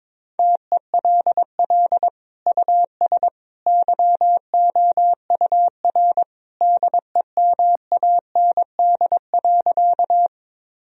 Morse code, heard as a keyed tone, about 22 wpm, 700 Hz. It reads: TELL US YOUR DEMAND.
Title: Sentences from Top 1000 Words - Review Part 4 - 22wpm